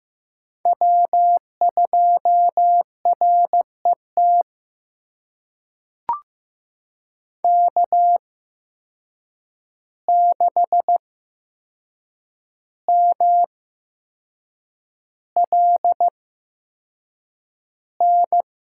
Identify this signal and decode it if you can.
Morse code, heard as a keyed tone, about 15 wpm, spaced as for 6 wpm, 700 Hz. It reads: W2RET K6MLN
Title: USA Call Signs - 8x Character Spacing - 15wpm